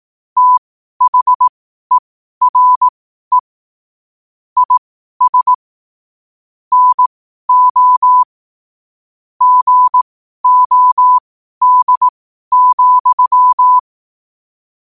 Morse code, heard as a keyed tone, about 18 wpm, spaced as for 12 wpm, 1000 Hz. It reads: THERE IS NO GOD,